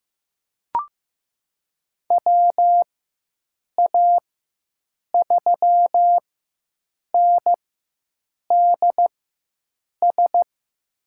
Morse code, heard as a keyed tone, about 15 wpm, spaced as for 7 wpm, 700 Hz. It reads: WA3NDS
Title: USA Call Signs - 6 Characters - 4x Character Spacing - 15wpm